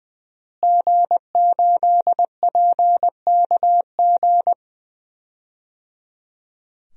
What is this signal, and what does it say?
Morse code, heard as a keyed tone, about 20 wpm, 700 Hz. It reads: G8PKG
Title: DXCC-100 Call Signs - 5 Characters - Repeated 3x - 20wpm